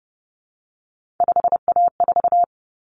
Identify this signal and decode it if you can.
Morse code, heard as a keyed tone, about 30 wpm, 700 Hz. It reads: 5A4
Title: Common Cut Numbers - 3-Digit 30wpm